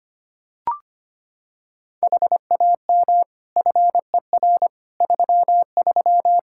Morse code, heard as a keyed tone, about 25 wpm, 700 Hz. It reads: HAM FER 33
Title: Practice QSO - Exchange 3 - More Info - 25wpm